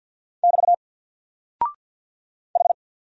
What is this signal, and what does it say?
Morse code, heard as a keyed tone, about 48 wpm, 700 Hz. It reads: <BT> H
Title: Single Character (Letter:Number:Punctuation:Prosign) Alt 50wpm